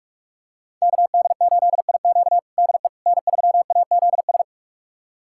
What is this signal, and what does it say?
Morse code, heard as a keyed tone, about 44 wpm, 700 Hz. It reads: KD8IQ DE N3AZS
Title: USA Callsign DE Callsign - 5 Characters - 45wpm